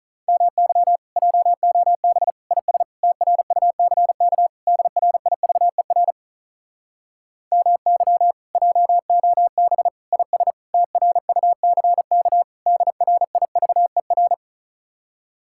Morse code, heard as a keyed tone, about 44 wpm, 700 Hz. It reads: MY JOB IS TRUCK DRIVER MY JOB IS TRUCK DRIVER